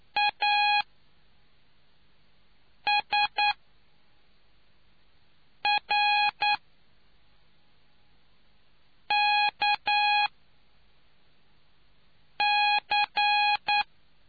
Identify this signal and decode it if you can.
Morse code, heard as a keyed tone, about 9 wpm, spaced as for 3 wpm, 830 Hz. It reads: ASRKC